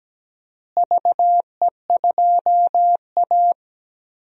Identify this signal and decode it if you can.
Morse code, heard as a keyed tone, about 17 wpm, 700 Hz. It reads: VE2A
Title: DXCC-100 Call Signs - 4 Characters - Repeated 3x - 17wpm